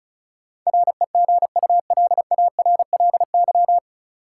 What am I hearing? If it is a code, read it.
Morse code, heard as a keyed tone, about 36 wpm, 700 Hz. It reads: REGULARLY
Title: Sentences from Top 1400 Words - New Word Drill - 35wpm